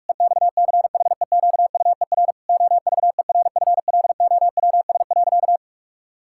Morse code, heard as a keyed tone, about 46 wpm, 700 Hz. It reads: EXCHEQUER OVERFLOWS.
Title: Pangram Challenge v2 - 45wpm